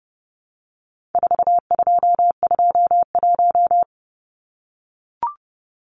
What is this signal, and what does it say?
Morse code, heard as a keyed tone, about 30 wpm, 700 Hz. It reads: 4221
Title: Numbers - 4-Digit 30wpm